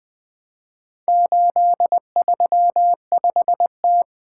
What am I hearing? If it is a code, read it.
Morse code, heard as a keyed tone, about 20 wpm, 700 Hz. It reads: 835T
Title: Four Letter-Number Combinations - 20wpm